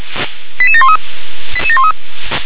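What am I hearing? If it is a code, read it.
Five-tone selective call (ZVEI), heard as quick runs of five tones, tones of about 70 ms. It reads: 88713
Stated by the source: ZVEI five-tone selective call received on SDR.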